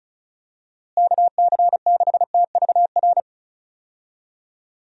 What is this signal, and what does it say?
Morse code, heard as a keyed tone, about 34 wpm, 700 Hz. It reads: KC6TVR
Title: USA Call Signs - 6 Characters - Repeated 3x - 35wpm